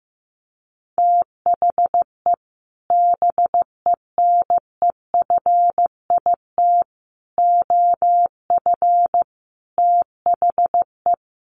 Morse code, heard as a keyed tone, about 14 wpm, 700 Hz. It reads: THE BENEFIT OF THE
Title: Sets of 5 Words Encore 15wpm